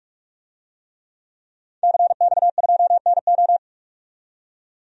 Morse code, heard as a keyed tone, about 47 wpm, 700 Hz. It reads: CX2NO
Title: DXCC-100 Call Signs - 5 Characters - 45wpm